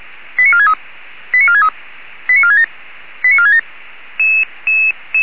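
Five-tone selective call (ZVEI), heard as quick runs of five tones, tones of about 70 ms. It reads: 89463 89463 89467 89467
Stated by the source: ZVEI five-tone selective call received on an SDR.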